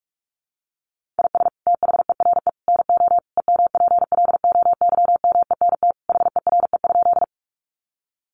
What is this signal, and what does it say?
Morse code, heard as a keyed tone, about 44 wpm, 700 Hz. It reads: IS THERE NO EMPLOYMENT HERE?
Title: Sentences from Top 1500 Words 45wpm